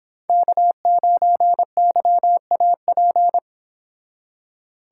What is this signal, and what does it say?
Morse code, heard as a keyed tone, about 30 wpm, 700 Hz. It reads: K9YAP